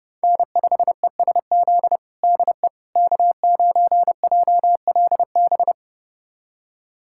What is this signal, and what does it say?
Morse code, heard as a keyed tone, about 30 wpm, 700 Hz. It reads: N5ESZ DE K9JLB
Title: USA Callsign DE Callsign - 5 Characters - 30wpm